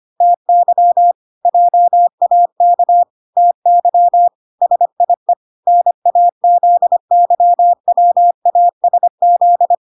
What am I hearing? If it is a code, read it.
Morse code, heard as a keyed tone, about 25 wpm, 700 Hz. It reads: TY JAK TY SIE NAZYWASZ